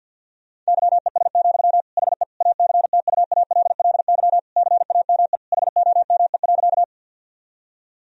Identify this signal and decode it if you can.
Morse code, heard as a keyed tone, about 51 wpm, 700 Hz. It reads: YES, HE ACTUALLY CAME HOME.
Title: Sentences from Top 1000 Words - Review Part 4 - 50wpm